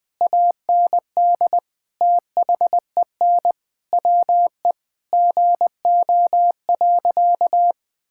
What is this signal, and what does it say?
Morse code, heard as a keyed tone, about 20 wpm, 700 Hz. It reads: AND THEN WE GO.
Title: Sentences with 2-Word Phrases - 20wpm